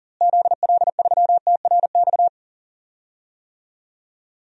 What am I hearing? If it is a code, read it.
Morse code, heard as a keyed tone, about 40 wpm, 700 Hz. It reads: ZL3TRX
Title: DXCC-100 Call Signs - 6 Characters - Repeated 3x - 40wpm